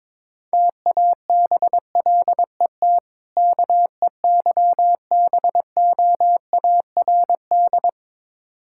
Morse code, heard as a keyed tone, about 22 wpm, 700 Hz. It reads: TABLET KEYBOARD
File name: Difficult Words with Hints - 22wpm